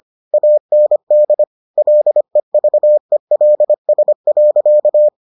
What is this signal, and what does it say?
Morse code, heard as a keyed tone, about 25 wpm, 600 Hz. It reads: AND LEVELS.